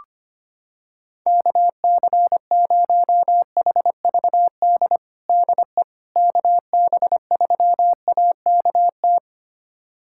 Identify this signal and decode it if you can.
Morse code, heard as a keyed tone, about 25 wpm, 700 Hz. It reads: KC0HVD DE KB3AKT
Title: USA Callsign DE Callsign 25wpm